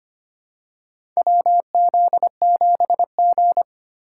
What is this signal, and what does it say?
Morse code, heard as a keyed tone, about 25 wpm, 700 Hz. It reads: WZ7G
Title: USA Call Signs - 4 Characters - 4x Character Spacing - 25wpm